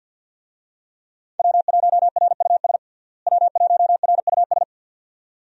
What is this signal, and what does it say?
Morse code, heard as a keyed tone, about 46 wpm, 700 Hz. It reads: W1RUS W1RUS
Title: USA Call Signs - 5 Characters - Repeated 2x - 50wpm